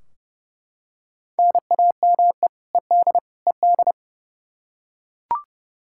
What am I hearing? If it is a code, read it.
Morse code, heard as a keyed tone, about 30 wpm, 700 Hz. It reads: NAME ED ED
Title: Practice QSO - Exchange 1 - Beginning - 30wpm